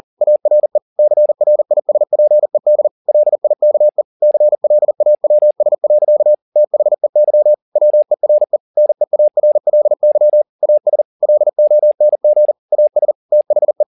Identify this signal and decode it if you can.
Morse code, heard as a keyed tone, about 40 wpm, 600 Hz. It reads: ARE CRISPED LIKE CLAWS. THEY WERE NEARLY AS LONG AS THE
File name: the old man and the sea 2 40wpm